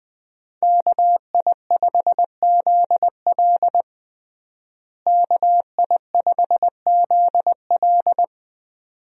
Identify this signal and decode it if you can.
Morse code, heard as a keyed tone, about 20 wpm, 700 Hz. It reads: KI5ZL KI5ZL
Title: USA Call Signs - 5 Characters - Repeated 2x - 20wpm